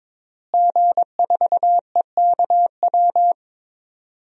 Morse code, heard as a keyed tone, about 22 wpm, 700 Hz. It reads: G4EKW
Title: DXCC-100 Call Signs - 5 Characters - Repeated 3x - 22wpm